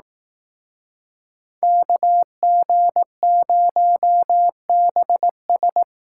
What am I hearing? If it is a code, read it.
Morse code, heard as a keyed tone, about 18 wpm, 700 Hz. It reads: KG0BS
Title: Speed Racing - USA 2x2 Call Signs 15wpm